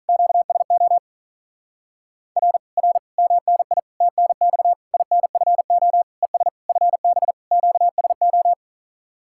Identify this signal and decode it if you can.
Morse code, heard as a keyed tone, about 46 wpm, 700 Hz. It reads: QSO R R MNI TNX INFO ES FB QSO